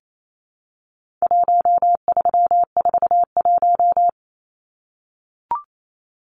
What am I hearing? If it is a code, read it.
Morse code, heard as a keyed tone, about 28 wpm, 700 Hz. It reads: 1341
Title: Numbers - 4-Digit 28wpm